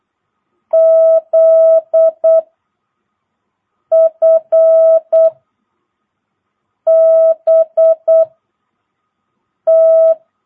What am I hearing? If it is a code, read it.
Morse code, heard as a keyed tone, about 8 wpm, 660 Hz. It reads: Z F B T
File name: jh8cbh / 2022cw